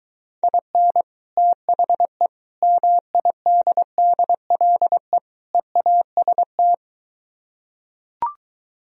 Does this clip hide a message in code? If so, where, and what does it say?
Yes: Morse code from 0.4 to 6.7 s, heard as a keyed tone, about 23 wpm, 700 Hz. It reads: IN THE MIDDLE EAST